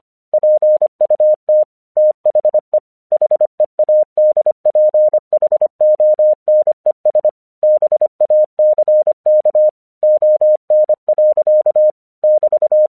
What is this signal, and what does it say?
Morse code, heard as a keyed tone, about 26 wpm, 610 Hz. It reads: PUT THE HEADPHONES BACK ON. <BT>
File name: 1 English 25wpm / AQC28